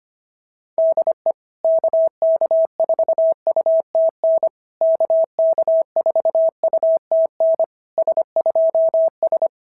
Morse code, heard as a keyed tone, about 25 wpm, 650 Hz. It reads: DE KK4UTN KK4UTN S2S
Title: SOTA QSO Practice 25wpm